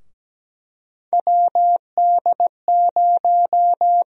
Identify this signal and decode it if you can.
Morse code, heard as a keyed tone, about 17 wpm, 700 Hz. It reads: WD0
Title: USA Call Sign Prefixes - Repeated 2x - 17wpm